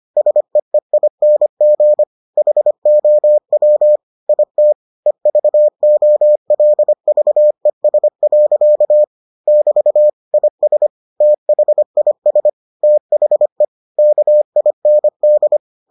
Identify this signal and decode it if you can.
Morse code, heard as a keyed tone, about 25 wpm, 600 Hz. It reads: SEEING HOW IT EVOLVES. <BT> IS THIS THE KIND